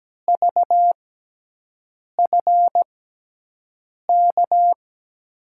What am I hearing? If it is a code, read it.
Morse code, heard as a keyed tone, about 17 wpm, spaced as for 6 wpm, 700 Hz. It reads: VFK
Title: USA Call Signs - 6x Character Spacing - 17wpm